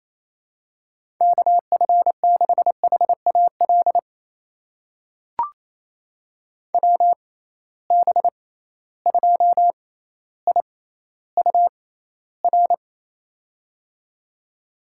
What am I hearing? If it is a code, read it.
Morse code, heard as a keyed tone, about 28 wpm, spaced as for 14 wpm, 700 Hz. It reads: KF6HAL WB2IUR
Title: USA Call Signs - 6 Characters - 6x Character Spacing - 28wpm